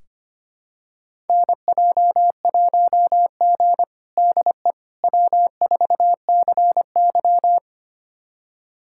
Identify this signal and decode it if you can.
Morse code, heard as a keyed tone, about 25 wpm, 700 Hz. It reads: NJ1G DE W4CY
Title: USA Callsign DE Callsign - 4 Characters - 25wpm